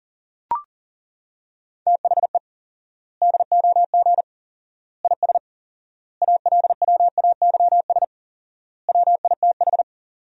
Morse code, heard as a keyed tone, about 40 wpm, 700 Hz. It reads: THE DOG IS ALWAYS WITH
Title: 4x Word-Spacing - Sentences from Top 400 Words - 40wpm